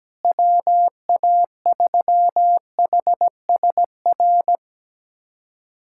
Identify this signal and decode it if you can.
Morse code, heard as a keyed tone, about 17 wpm, 700 Hz. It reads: WA3HSR
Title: USA Callsign DE Callsign 17wpm